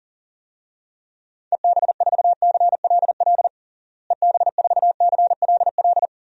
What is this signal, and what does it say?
Morse code, heard as a keyed tone, about 37 wpm, 700 Hz. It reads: EB4CLL EB4CLL
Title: DXCC-100 Call Signs - 6 Characters - Repeated 3x - 40wpm